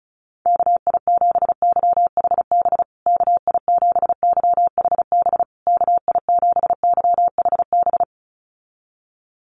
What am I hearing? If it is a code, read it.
Morse code, heard as a keyed tone, about 34 wpm, 700 Hz. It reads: KI7YHB KI7YHB KI7YHB
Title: USA Call Signs Repeated 3x 35wpm